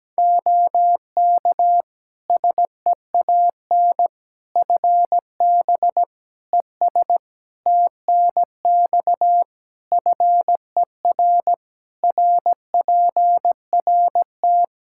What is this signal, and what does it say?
Morse code, heard as a keyed tone, about 17 wpm, 700 Hz. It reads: OK SEAN FB ES TNX FER RPRT
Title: Practice QSO - Exchange 2 - Info - 17wpm